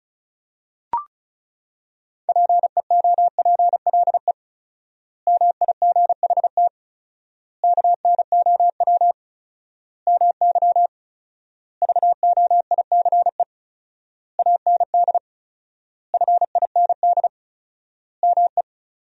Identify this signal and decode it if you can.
Morse code, heard as a keyed tone, about 34 wpm, 700 Hz. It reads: PEOPLE MIGHT KNOW MY VOICE AND FIND ME
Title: 4x Word-Spacing - Sentences from Top 500 Words - 35wpm